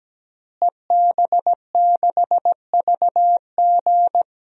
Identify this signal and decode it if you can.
Morse code, heard as a keyed tone, about 17 wpm, 700 Hz. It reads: EB6VG
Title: DXCC-100 Call Signs - 5 Characters - Repeated 2x - 17wpm